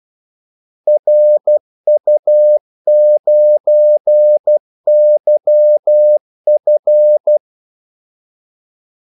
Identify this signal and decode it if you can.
Morse code, heard as a keyed tone, about 12 wpm, 600 Hz. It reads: RU9YF
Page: 2 ekstra ordmellemrum mellem kaldesignalerne